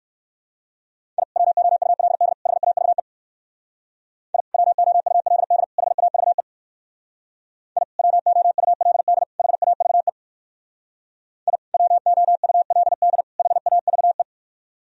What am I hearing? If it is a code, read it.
Morse code, heard as a keyed tone, about 58 wpm, 700 Hz. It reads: I WOULD HAVE I WOULD HAVE I WOULD HAVE I WOULD HAVE